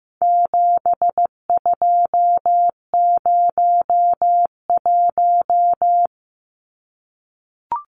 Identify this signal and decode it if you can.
Morse code, heard as a keyed tone, about 15 wpm, 700 Hz. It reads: 7201
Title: Numbers - 4-Digit 15wpm